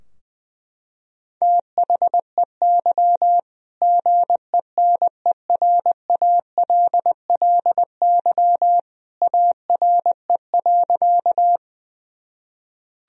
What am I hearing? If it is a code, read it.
Morse code, heard as a keyed tone, about 20 wpm, 700 Hz. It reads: THEY GENERALLY ARE.
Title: Sentences from Top 1000 Words - Review Part 4 - 20wpm